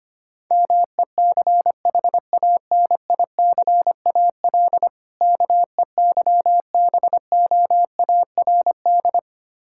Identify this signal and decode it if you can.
Morse code, heard as a keyed tone, about 25 wpm, 690 Hz. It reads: MECHANICAL KEYBOARD
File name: practice / 25wpm / Difficult Words - Speed-Racing with Hints - 25wpm